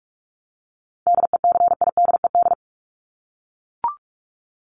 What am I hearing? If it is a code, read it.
Morse code, heard as a keyed tone, about 44 wpm, 700 Hz. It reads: DECIDED
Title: ED Suffix - Top 100 Words with Base Words 45wpm